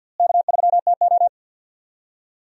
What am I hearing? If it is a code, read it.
Morse code, heard as a keyed tone, about 46 wpm, 700 Hz. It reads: K3TO
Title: USA Call Signs - 4 Characters - Repeated 3x - 50wpm